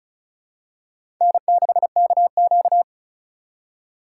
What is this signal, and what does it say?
Morse code, heard as a keyed tone, about 36 wpm, 690 Hz. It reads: N6KQ
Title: USA Call Signs - 4 Characters - 3x Character Spacing - 35wpm